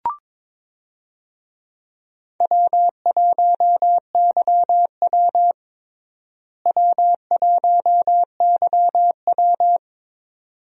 Morse code, heard as a keyed tone, about 22 wpm, 700 Hz. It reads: W1YW W1YW
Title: USA Call Signs - 4 Characters - Repeated 2x - 22wpm